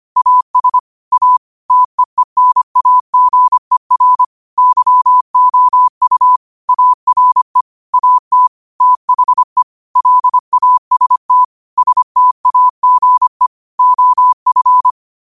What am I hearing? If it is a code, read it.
Morse code, heard as a keyed tone, about 25 wpm, 1000 Hz. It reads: AS A TEENAGER YOU ARE AT THE LAST STAGE OF
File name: qotdfiles / 25wpm